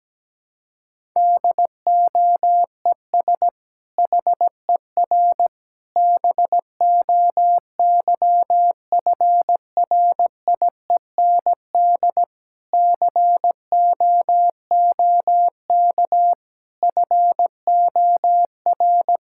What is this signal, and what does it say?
Morse code, heard as a keyed tone, about 17 wpm, 700 Hz. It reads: DOES HER BOYFRIEND COOK FOR
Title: Sentences from Top 1000 Words - Review Part 5 - 17wpm